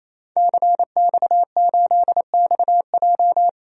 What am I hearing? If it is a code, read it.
Morse code, heard as a keyed tone, about 28 wpm, 700 Hz. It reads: CX8XJ
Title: DXCC-100 Call Signs - 5 Characters - 28wpm